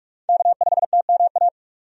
Morse code, heard as a keyed tone, about 44 wpm, 700 Hz. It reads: K5TMA
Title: USA Call Signs - 5 Characters - 45wpm